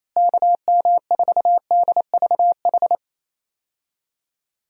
Morse code, heard as a keyed tone, about 28 wpm, 700 Hz. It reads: KM4DVH
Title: USA Call Signs - 6 Characters - Repeated 3x - 28wpm